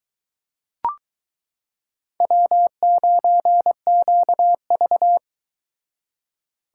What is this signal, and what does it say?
Morse code, heard as a keyed tone, about 22 wpm, 700 Hz. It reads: W9QV